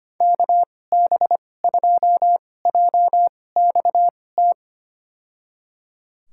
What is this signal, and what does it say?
Morse code, heard as a keyed tone, about 25 wpm, spaced as for 18 wpm, 700 Hz. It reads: KB2JXT
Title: USA Call Signs - 6 Characters - 2x Character Spacing - 25wpm